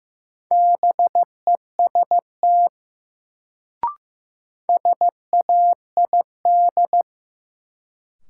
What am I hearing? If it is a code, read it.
Morse code, heard as a keyed tone, about 15 wpm, 700 Hz. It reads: BEST SAID
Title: Sets of 2 Words - New Word Drill - 15wpm